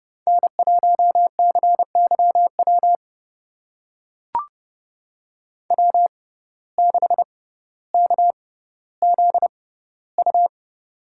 Morse code, heard as a keyed tone, about 30 wpm, spaced as for 15 wpm, 700 Hz. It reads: N1CYW W6KZU
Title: USA Call Signs - 5 Characters - 6x Character Spacing - 30wpm